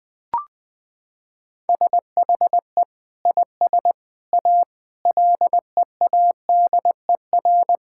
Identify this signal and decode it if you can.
Morse code, heard as a keyed tone, about 20 wpm, 700 Hz. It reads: SHE IS A LEADER
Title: Sentences with 5-Word Phrases - 20wpm